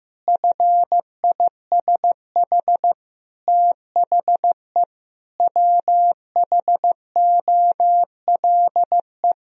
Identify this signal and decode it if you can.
Morse code, heard as a keyed tone, about 15 wpm, 700 Hz. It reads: FISH THE WHOLE
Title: Sentences from Top 1000 Words - Review Part 1 - 15wpm